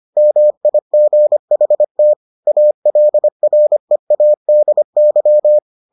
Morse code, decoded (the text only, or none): MIGHT ALREADY